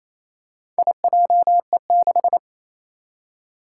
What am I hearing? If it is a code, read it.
Morse code, heard as a keyed tone, about 28 wpm, 700 Hz. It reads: IJE6
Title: Four Challenging Letter-Number Combinations 28wpm